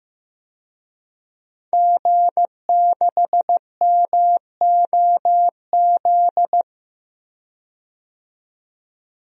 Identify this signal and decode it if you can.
Morse code, heard as a keyed tone, about 15 wpm, 700 Hz. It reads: G6MOZ
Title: DXCC-100 Call Signs - 5 Characters - 15wpm